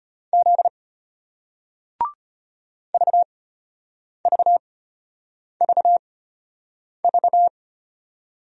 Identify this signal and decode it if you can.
Morse code, heard as a keyed tone, about 35 wpm, spaced as for 8 wpm, 700 Hz. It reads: Z VVVV